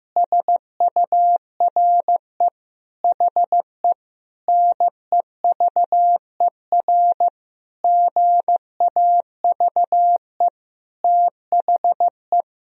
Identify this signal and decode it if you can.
Morse code, heard as a keyed tone, about 15 wpm, 700 Hz. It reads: SURE HE NEVER GAVE THE
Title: Sentences from Top 1000 Words - Review Part 2 - 15wpm